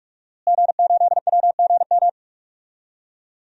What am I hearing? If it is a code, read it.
Morse code, heard as a keyed tone, about 42 wpm, 700 Hz. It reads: G8WGM
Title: DXCC-100 Call Signs - 5 Characters - Repeated 3x - 45wpm